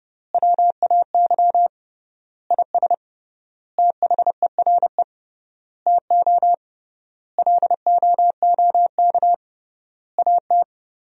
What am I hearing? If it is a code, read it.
Morse code, heard as a keyed tone, about 30 wpm, 700 Hz. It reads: WAY IS THERE TO LOOK AT
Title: 3x Word-Spacing - Sentences from Top 100 Words - 30wpm